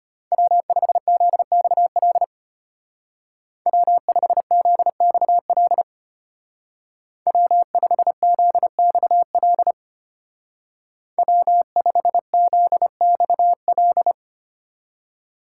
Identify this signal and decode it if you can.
Morse code, heard as a keyed tone, about 30 wpm, 700 Hz. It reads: W5ZXL W5ZXL W5ZXL W5ZXL
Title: Speed Racing - USA 1x3 Call Signs 25wpm